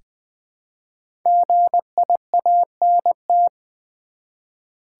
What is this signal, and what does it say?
Morse code, heard as a keyed tone, about 20 wpm, 700 Hz. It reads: GIANT
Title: Build A Word - 1 Letter At A Time - 5-Letter Words - 20wpm